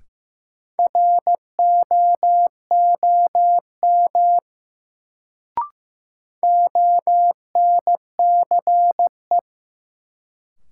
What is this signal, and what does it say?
Morse code, heard as a keyed tone, about 15 wpm, 700 Hz. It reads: ROOM ONCE
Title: Sentences from Top 300 Words - New Word Drill - 15wpm